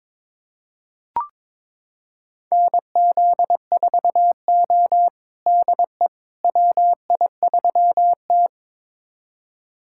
Morse code, heard as a keyed tone, about 22 wpm, 700 Hz. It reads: NZ4O DE WI3T
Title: USA Callsign DE Callsign - 4 Characters - 22wpm